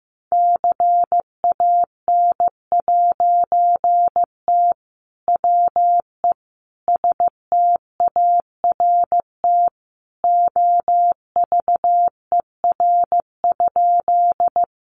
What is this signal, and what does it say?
Morse code, heard as a keyed tone, about 15 wpm, 700 Hz. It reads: CAN'T WE START OVER?
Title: A Sound Of Thunder 15wpm